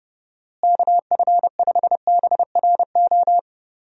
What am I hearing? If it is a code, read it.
Morse code, heard as a keyed tone, about 30 wpm, 700 Hz. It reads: KF5BRO
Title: USA Call Signs - 6 Characters - Repeated 3x - 30wpm